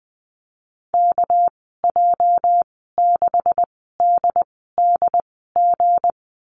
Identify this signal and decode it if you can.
Morse code, heard as a keyed tone, about 20 wpm, spaced as for 14 wpm, 700 Hz. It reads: KJ6DDG
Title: 245 Lesson 39g - Letter z - Call Signs 20wpm